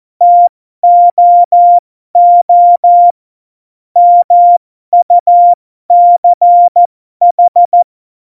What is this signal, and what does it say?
Morse code, heard as a keyed tone, about 14 wpm, spaced as for 12 wpm, 700 Hz. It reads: TOO MUCH
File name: Morsekurs / Witze